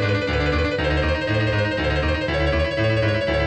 There is no code.